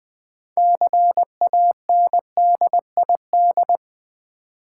Morse code, heard as a keyed tone, about 20 wpm, 700 Hz. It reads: CANDID
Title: Build A Word - 3 Letters At A Time - 9-Letter Words - 20wpm